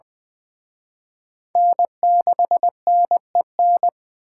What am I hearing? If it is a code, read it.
Morse code, heard as a keyed tone, about 24 wpm, 700 Hz. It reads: N6NEN
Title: Speed Racing - USA 1x3 Call Signs 20wpm